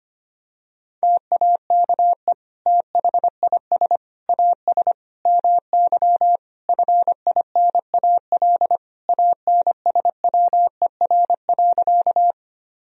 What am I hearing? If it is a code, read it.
Morse code, heard as a keyed tone, about 25 wpm, 700 Hz. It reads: TAKE THIS AS MY FINAL ANSWER.
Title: Sentences from Top 1000 Words - Review Part 2 - 25wpm